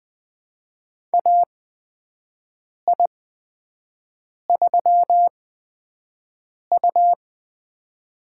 Morse code, heard as a keyed tone, about 21 wpm, spaced as for 5 wpm, 700 Hz. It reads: AI3U
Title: USA Call Signs - 4 Characters - 8x Character Spacing - 20wpm